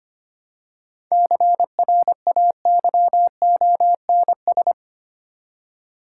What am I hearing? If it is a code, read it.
Morse code, heard as a keyed tone, about 25 wpm, 690 Hz. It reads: CRAYONS
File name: practice / 25wpm / Easy Words with Hints - 25wpm